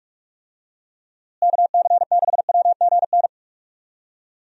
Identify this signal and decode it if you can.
Morse code, heard as a keyed tone, about 45 wpm, 700 Hz. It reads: KC6WGN
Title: USA Call Signs - 6 Characters - Repeated 2x - 45wpm